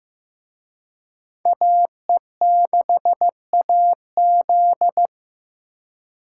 Morse code, heard as a keyed tone, about 18 wpm, 700 Hz. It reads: AE6AZ
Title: Speed Racing - USA 2x2 Call Signs 15wpm